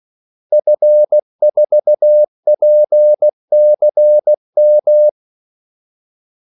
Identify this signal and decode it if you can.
Morse code, heard as a keyed tone, about 16 wpm, 600 Hz. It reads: F4PCM
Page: Kaldesignaler 16 wpm